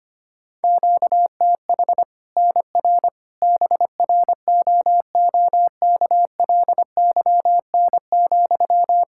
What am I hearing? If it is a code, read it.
Morse code, heard as a keyed tone, about 25 wpm, 700 Hz. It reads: QTH NR BROOKLYN,
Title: Practice QSO - Exchange 1 - Beginning - 25wpm